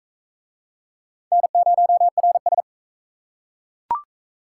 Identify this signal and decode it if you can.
Morse code, heard as a keyed tone, about 37 wpm, 700 Hz. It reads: N0RS